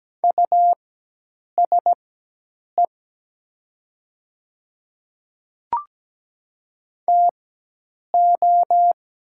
Morse code, heard as a keyed tone, about 17 wpm, spaced as for 8 wpm, 700 Hz. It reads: USE TO
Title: 4x Character Spacing - Top 300 Words - 17wpm